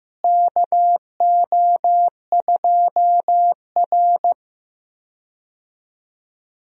Morse code, heard as a keyed tone, about 15 wpm, 700 Hz. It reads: KO2R
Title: USA Call Signs - 4 Characters - Repeated 3x - 15wpm